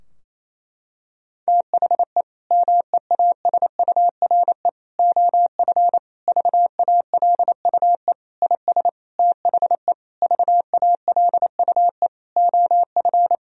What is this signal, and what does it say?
Morse code, heard as a keyed tone, about 28 wpm, 700 Hz. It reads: THE MEASURE OF VALUE IS THE VALUE OF
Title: Sentences from Top 1000 Words - Review Part 3 - 28wpm